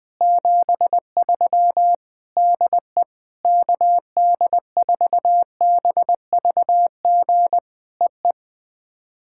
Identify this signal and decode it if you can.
Morse code, heard as a keyed tone, about 20 wpm, 700 Hz. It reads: 73 DE KD4BVG EE
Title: POTA QSO Practice - 20wpm